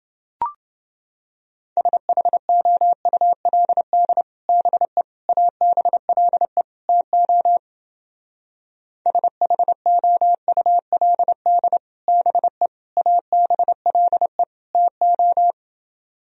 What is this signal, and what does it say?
Morse code, heard as a keyed tone, about 29 wpm, 700 Hz. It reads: SHOULD BE ABLE TO SHOULD BE ABLE TO